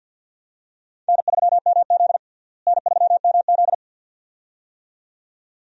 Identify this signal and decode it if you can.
Morse code, heard as a keyed tone, about 52 wpm, 700 Hz. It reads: N3MZ N3MZ
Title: USA Call Signs - 4 Characters - Repeated 2x - 50wpm